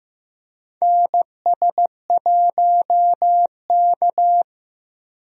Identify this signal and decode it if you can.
Morse code, heard as a keyed tone, about 15 wpm, 700 Hz. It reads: NS1K
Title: USA Call Signs - 4 Characters - Repeated 3x - 15wpm